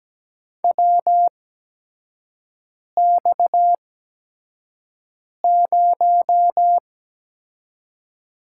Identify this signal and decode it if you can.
Morse code, heard as a keyed tone, about 17 wpm, spaced as for 5 wpm, 700 Hz. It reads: WX0
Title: USA Call Signs - 4 Characters - 8x Character Spacing - 17wpm